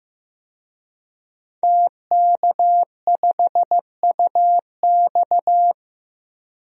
Morse code, heard as a keyed tone, about 15 wpm, 700 Hz. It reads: TK5UX
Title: DXCC-100 Call Signs - 5 Characters - Repeated 3x - 15wpm